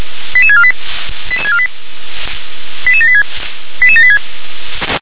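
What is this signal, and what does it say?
Five-tone selective call (ZVEI), heard as quick runs of five tones, tones of about 70 ms. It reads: 88648 88786 88786
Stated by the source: ZVEI five-tone selective call received with an SDR.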